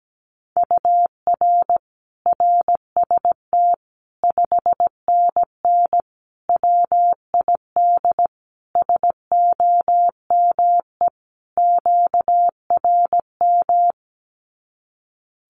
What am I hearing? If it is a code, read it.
Morse code, heard as a keyed tone, about 17 wpm, 700 Hz. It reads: UR RST 5NN WID SOME QRM
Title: Common QSO Phrases 17wpm